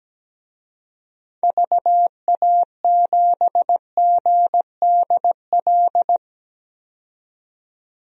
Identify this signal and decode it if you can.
Morse code, heard as a keyed tone, about 17 wpm, 700 Hz. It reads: VA7GDL
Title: DXCC-100 Call Signs - 6 Characters - 17wpm